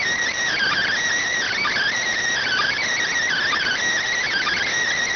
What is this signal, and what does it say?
Emergency Alert System (SAME) tones (header bursts), heard as harsh buzzing bursts. It is heard from 0.5 to 4.7 s.